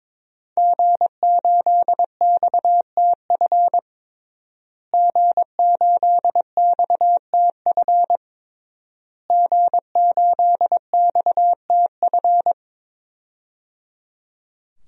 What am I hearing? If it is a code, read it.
Morse code, heard as a keyed tone, about 21 wpm, 700 Hz. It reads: G8XTF G8XTF G8XTF
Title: DXCC-100 Call Signs - 5 Characters - Repeated 3x - 22wpm